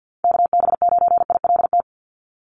Morse code, heard as a keyed tone, about 50 wpm, 700 Hz. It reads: KB9ILT
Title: NA SOTA Chaser Callsigns 50wpm